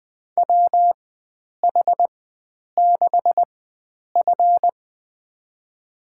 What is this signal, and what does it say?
Morse code, heard as a keyed tone, about 20 wpm, spaced as for 9 wpm, 700 Hz. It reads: WH6F
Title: USA Call Signs - 4 Characters - 4x Character Spacing - 20wpm